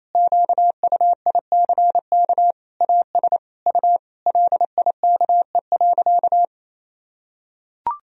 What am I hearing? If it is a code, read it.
Morse code, heard as a keyed tone, about 28 wpm, 700 Hz. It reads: QUICK AS U LIKE.
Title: Sentences from Top 500 Words and Abbreviations 28wpm